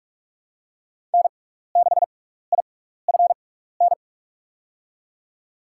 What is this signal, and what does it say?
Morse code, heard as a keyed tone, about 44 wpm, spaced as for 15 wpm, 690 Hz. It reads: N6IFN
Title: USA Call Signs - 5 Characters - 6x Character Spacing - 45wpm